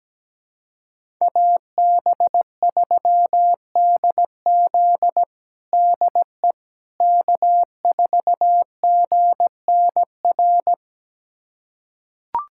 Morse code, heard as a keyed tone, about 17 wpm, 700 Hz. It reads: AB3DZ DE K4GNR
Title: USA Callsign DE Callsign - 5 Characters - 17wpm